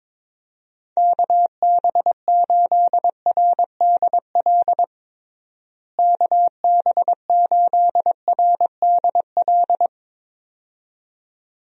Morse code, heard as a keyed tone, about 22 wpm, 700 Hz. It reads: KB8RDL KB8RDL
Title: USA Call Signs - 6 Characters - Repeated 2x - 22wpm